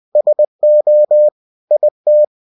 Morse code, heard as a keyed tone, about 20 wpm, 600 Hz. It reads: SO IT
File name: cw-books / winnie-the-pooh / winnie-20wpm-page-0058